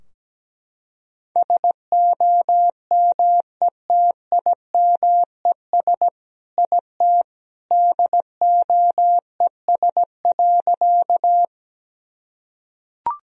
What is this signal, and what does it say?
Morse code, heard as a keyed tone, about 17 wpm, 700 Hz. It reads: SOMETIMES IT DOES.
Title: Sentences from Top 1000 Words - Review Part 3 - 17wpm